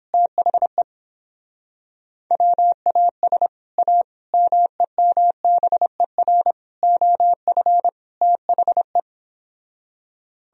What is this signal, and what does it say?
Morse code, heard as a keyed tone, about 31 wpm, 700 Hz. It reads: THE WAS A MEMBER OF THE